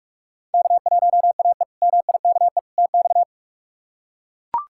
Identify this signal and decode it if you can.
Morse code, heard as a keyed tone, about 41 wpm, 700 Hz. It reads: K1AE MIKE TX
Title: K1USN SST QSO Practice - 45wpm